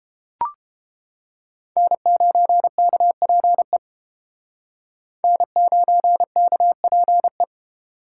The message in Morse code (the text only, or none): N9KPE N9KPE